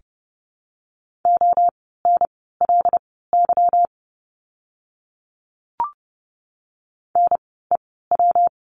Morse code, heard as a keyed tone, about 30 wpm, spaced as for 17 wpm, 700 Hz. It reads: ONLY NEW
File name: course / 30:10wpm / 230 Lesson 37f - Letter x - Words 30wpm